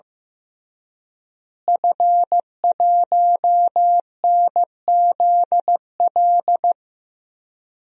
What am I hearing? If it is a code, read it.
Morse code, heard as a keyed tone, about 15 wpm, 700 Hz. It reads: F1NZL
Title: DXCC-100 Call Signs - 5 Characters - Repeated 3x - 15wpm